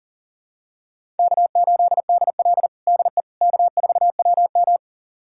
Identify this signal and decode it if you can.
Morse code, heard as a keyed tone, about 37 wpm, 700 Hz. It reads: K8DL DE K4WM
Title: USA Callsign DE Callsign - 4 Characters - 40wpm